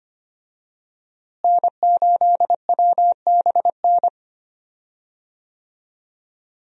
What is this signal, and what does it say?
Morse code, heard as a keyed tone, about 25 wpm, 700 Hz. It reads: N8WBN
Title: USA Call Signs - 5 Characters - Repeated 2x - 25wpm